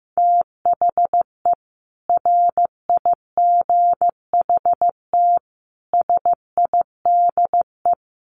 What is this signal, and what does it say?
Morse code, heard as a keyed tone, about 15 wpm, 700 Hz. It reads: THE RIGHT SIDE
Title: Sets of 5 Words Encore 15wpm